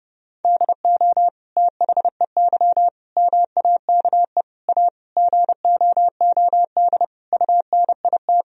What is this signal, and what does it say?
Morse code, heard as a keyed tone, about 30 wpm, 700 Hz. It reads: DO THEY MAKE A GOOD UNIT
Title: Sentences from Top 1000 Words - Review Part 2 - 30wpm